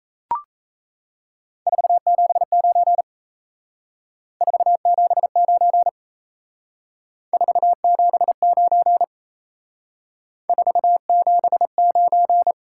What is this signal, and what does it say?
Morse code, heard as a keyed tone, about 35 wpm, 700 Hz. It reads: 479 479 479 479